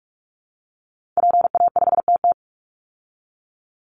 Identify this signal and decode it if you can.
Morse code, heard as a keyed tone, about 44 wpm, 700 Hz. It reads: PA5TT
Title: Call Signs Repeated 2x - 45wpm